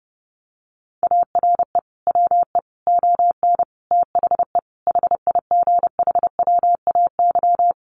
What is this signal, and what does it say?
Morse code, heard as a keyed tone, about 30 wpm, 700 Hz. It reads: ARE WE ON THE HIGHWAY
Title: Sentences from Top 1400 Words 30wpm